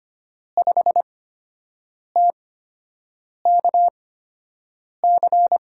Morse code, heard as a keyed tone, about 25 wpm, spaced as for 7 wpm, 700 Hz. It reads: 5TKC
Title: USA Call Signs - 5 Characters - 8x Character Spacing - 25wpm